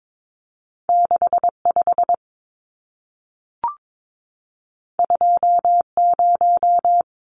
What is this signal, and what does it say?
Morse code, heard as a keyed tone, about 22 wpm, 700 Hz. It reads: 65 20
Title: Numbers - 2-Digit 22wpm